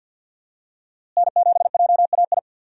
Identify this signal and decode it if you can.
Morse code, heard as a keyed tone, about 52 wpm, 700 Hz. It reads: N7JAI
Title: USA Call Signs - 5 Characters - Repeated 3x - 50wpm